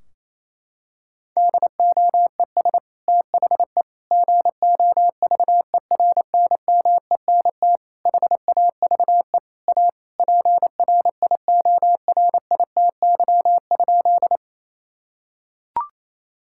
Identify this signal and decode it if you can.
Morse code, heard as a keyed tone, about 28 wpm, 700 Hz. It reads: DOES THE GOVERNMENT HAVE A PRIORITY?
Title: Sentences from Top 1600 Words 28wpm